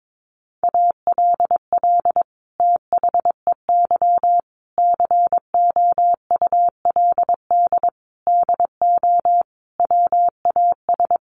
Morse code, heard as a keyed tone, about 22 wpm, 700 Hz. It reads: ALL THEY COULD DO WAS
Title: Sentences from Top 1300 Words 22wpm